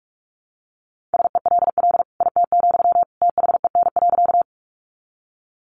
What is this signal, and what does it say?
Morse code, heard as a keyed tone, about 44 wpm, 700 Hz. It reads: SELL IT, THEN.
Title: Sentences from Top 1100 Words 45wpm